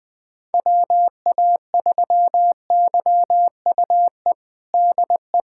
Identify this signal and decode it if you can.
Morse code, heard as a keyed tone, about 21 wpm, 700 Hz. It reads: WA3YUE DE
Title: USA Callsign DE Callsign 20wpm